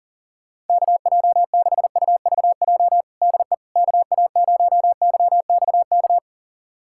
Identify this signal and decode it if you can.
Morse code, heard as a keyed tone, about 40 wpm, 700 Hz. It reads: KJ6UVJ DE KA0YXK
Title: USA Callsign DE Callsign - 6 Characters - 40wpm